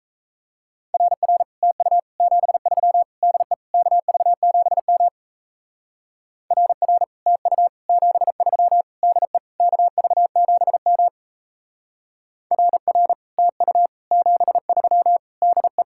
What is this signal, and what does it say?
Morse code, heard as a keyed tone, about 34 wpm, 700 Hz. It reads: RR TU 73 DE KV7M RR TU 73 DE KV7M RR TU 73 DE
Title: Speed-Racing Common QSO Phrases Encore - 28wpm